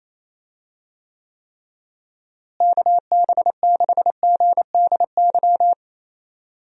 Morse code, heard as a keyed tone, about 28 wpm, 700 Hz. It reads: KB6GDY
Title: USA Call Signs - 6 Characters - Repeated 3x - 28wpm